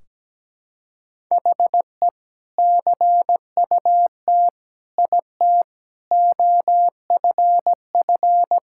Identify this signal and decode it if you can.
Morse code, heard as a keyed tone, about 17 wpm, 700 Hz. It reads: HE CUT IT OFF
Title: Sentences from Top 1600 Words 17wpm